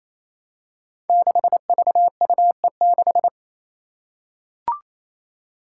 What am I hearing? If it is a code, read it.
Morse code, heard as a keyed tone, about 28 wpm, 700 Hz. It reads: 6VUE6
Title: Five Challenging Letter-Number Combinations 28wpm